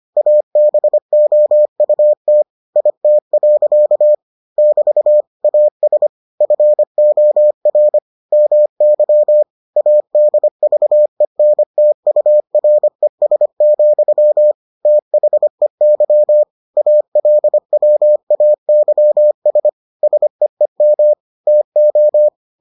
Morse code, heard as a keyed tone, about 25 wpm, 600 Hz. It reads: ABOUT IT. <BT> AS FOR MY ADVENTURES, THEY ALWAYS SEEM TO